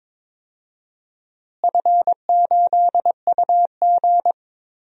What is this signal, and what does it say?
Morse code, heard as a keyed tone, about 22 wpm, 700 Hz. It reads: F8UG
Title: DXCC-100 Call Signs - 4 Characters - Repeated 2x - 22wpm